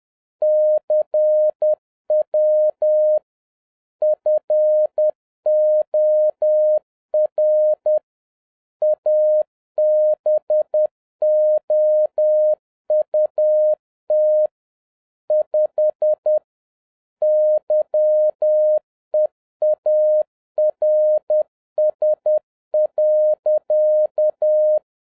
Morse code, heard as a keyed tone, about 10 wpm, 610 Hz. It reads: CW FOR ABOUT 5 YEARS.